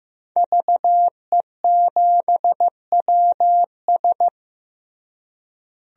Morse code, heard as a keyed tone, about 15 wpm, 700 Hz. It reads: VE7WS
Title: DXCC-100 Call Signs - 5 Characters - Repeated 2x - 15wpm